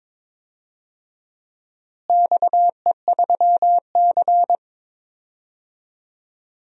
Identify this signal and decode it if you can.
Morse code, heard as a keyed tone, about 21 wpm, 700 Hz. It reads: XE3C
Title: DXCC-100 Call Signs - 4 Characters - Repeated 2x - 22wpm